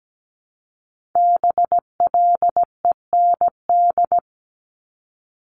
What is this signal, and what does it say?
Morse code, heard as a keyed tone, about 17 wpm, 700 Hz. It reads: BLEND
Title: Multi-letter Phonemes 17wpm